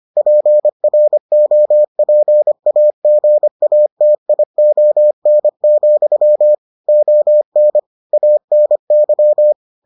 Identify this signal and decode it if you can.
Morse code, heard as a keyed tone, about 25 wpm, 600 Hz. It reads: PROPAGATION, ON ANY